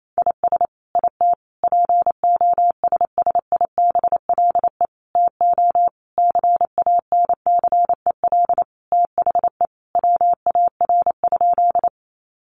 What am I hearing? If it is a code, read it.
Morse code, heard as a keyed tone, about 28 wpm, 700 Hz. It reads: IS IT POSSIBLE TO CANCEL THE WAR?
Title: Sentences from Top 1200 Words 28wpm